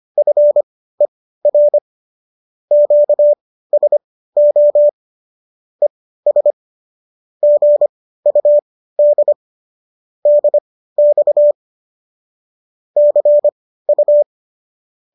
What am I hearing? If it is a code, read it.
Morse code, heard as a keyed tone, about 25 wpm, spaced as for 15 wpm, 590 Hz. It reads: FER QSO ES GUD DX CU